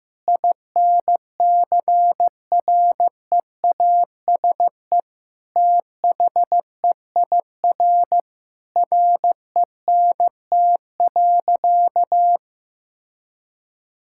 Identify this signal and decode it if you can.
Morse code, heard as a keyed tone, about 16 wpm, 700 Hz. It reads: INCREASE THEIR RENT.
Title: Sentences from Top 1000 Words - Review Part 5 - 15wpm